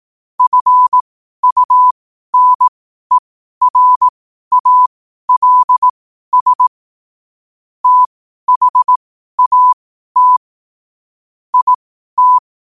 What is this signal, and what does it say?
Morse code, heard as a keyed tone, about 18 wpm, spaced as for 12 wpm, 1000 Hz. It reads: FUNERALS THAT IT
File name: qotdfiles / 12wpm